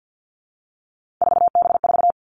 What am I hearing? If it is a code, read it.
Morse code, heard as a keyed tone, about 50 wpm, 700 Hz. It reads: 4B4
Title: Three Challenging Letter-Number Combinations 50wpm